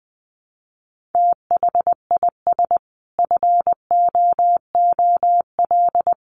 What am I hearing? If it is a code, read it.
Morse code, heard as a keyed tone, about 20 wpm, 700 Hz. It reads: THIS FOOL
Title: A Sound Of Thunder 20wpm